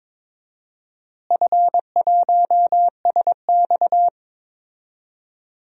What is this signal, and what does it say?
Morse code, heard as a keyed tone, about 22 wpm, 700 Hz. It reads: F1SX
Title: DXCC-100 Call Signs - 4 Characters - Repeated 3x - 22wpm